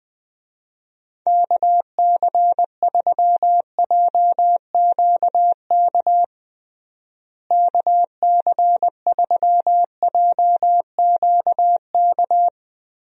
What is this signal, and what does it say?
Morse code, heard as a keyed tone, about 20 wpm, 700 Hz. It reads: KC3JQK KC3JQK
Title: USA Call Signs - 6 Characters - Repeated 3x - 20wpm